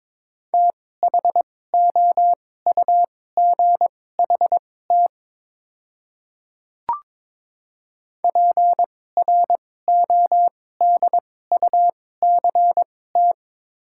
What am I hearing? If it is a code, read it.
Morse code, heard as a keyed tone, about 21 wpm, spaced as for 16 wpm, 690 Hz. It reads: THOUGHT PRODUCT
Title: 2x Character Spacing - Top 500 Words - 22wpm